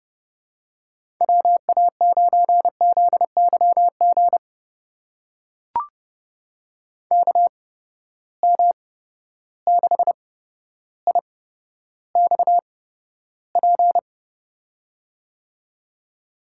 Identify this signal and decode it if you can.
Morse code, heard as a keyed tone, about 30 wpm, spaced as for 13 wpm, 700 Hz. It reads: WA9ZYG KM6IXP
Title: USA Call Signs - 6 Characters - 8x Character Spacing - 30wpm